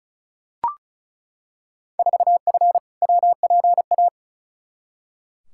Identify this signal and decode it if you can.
Morse code, heard as a keyed tone, about 36 wpm, 700 Hz. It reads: 4F WPA
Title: Field Day Exchanges - 35wpm